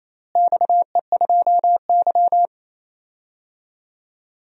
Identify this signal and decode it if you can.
Morse code, heard as a keyed tone, about 28 wpm, 700 Hz. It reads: XE2Y
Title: DXCC-100 Call Signs - 4 Characters - Repeated 2x - 28wpm